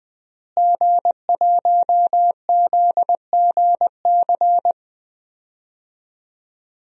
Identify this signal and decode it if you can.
Morse code, heard as a keyed tone, about 20 wpm, 700 Hz. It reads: G1ZGC
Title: DXCC-100 Call Signs - 5 Characters - Repeated 3x - 20wpm